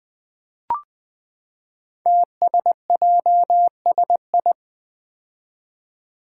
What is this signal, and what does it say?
Morse code, heard as a keyed tone, about 20 wpm, 700 Hz. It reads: TSJSI
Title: Five Challenging Letter-Number Combinations 20wpm